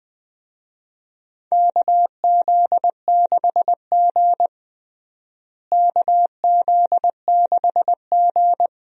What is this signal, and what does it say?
Morse code, heard as a keyed tone, about 20 wpm, 700 Hz. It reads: KZ6G KZ6G
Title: USA Call Signs - 4 Characters - Repeated 2x - 20wpm